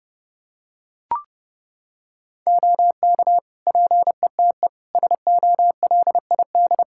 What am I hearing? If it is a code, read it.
Morse code, heard as a keyed tone, about 30 wpm, 700 Hz. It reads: OK PETE SOLID
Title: Practice QSO - Exchange 3 - More Info - 30wpm